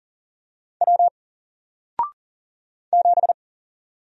Morse code, heard as a keyed tone, about 40 wpm, 700 Hz. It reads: W 7
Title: Single Letter-Number 40wpm